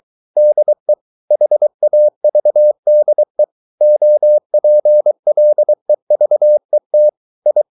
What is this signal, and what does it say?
Morse code, heard as a keyed tone, about 23 wpm, 600 Hz. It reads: DE HAVDE OPLEVET I
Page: Ferie 23 wpm